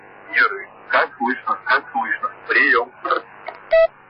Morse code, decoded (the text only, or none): none